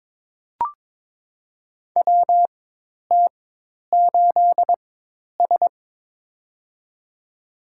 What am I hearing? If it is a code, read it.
Morse code, heard as a keyed tone, about 21 wpm, spaced as for 10 wpm, 700 Hz. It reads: WT8S
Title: USA Call Signs - 4 Characters - 4x Character Spacing - 22wpm